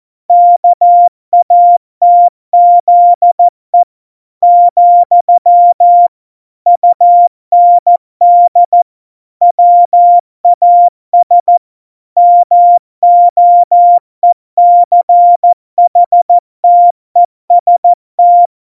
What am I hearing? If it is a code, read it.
Morse code, heard as a keyed tone, about 14 wpm, 700 Hz. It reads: KATZE , UND WAS MOECHTEST
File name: Morsekurs / Witze / Witz 03 April 2018, 70 F80